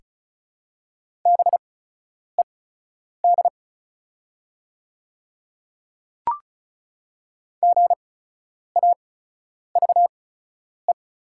Morse code, heard as a keyed tone, about 36 wpm, spaced as for 10 wpm, 700 Hz. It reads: BED GAVE
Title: 8x Character Spacing - Top 500 Words - 35wpm